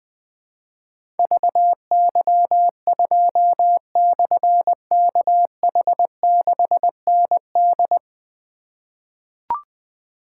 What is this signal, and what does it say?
Morse code, heard as a keyed tone, about 20 wpm, 700 Hz. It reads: VY2/KH6ND
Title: Difficult Call Signs 20wpm